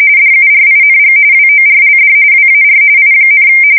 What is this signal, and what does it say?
RTTY, about 45.45 baud. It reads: DURING WORLD WAR I,
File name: fsk170 / usnavy4